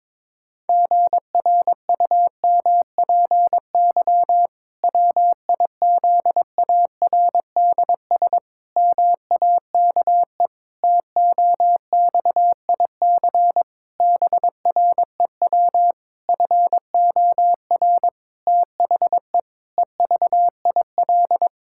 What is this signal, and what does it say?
Morse code, heard as a keyed tone, about 22 wpm, 700 Hz. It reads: GRUMPY WIZARDS MAKE TOXIC BREW FOR THE EVIL
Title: Pangram Challenge v2 - 22wpm